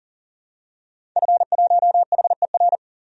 Morse code, heard as a keyed tone, about 37 wpm, 700 Hz. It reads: F1HER
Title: DXCC-100 Call Signs - 5 Characters - 40wpm